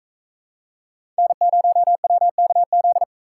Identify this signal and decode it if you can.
Morse code, heard as a keyed tone, about 43 wpm, 700 Hz. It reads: N0WKZ